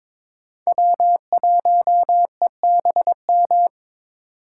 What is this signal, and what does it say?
Morse code, heard as a keyed tone, about 22 wpm, 700 Hz. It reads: W1EBM
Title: K1USN SST QSO Practice - 22wpm